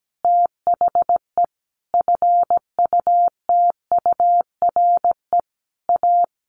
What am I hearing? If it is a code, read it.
Morse code, heard as a keyed tone, about 17 wpm, 700 Hz. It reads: THE FUTURE A
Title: Easy Phrases 17wpm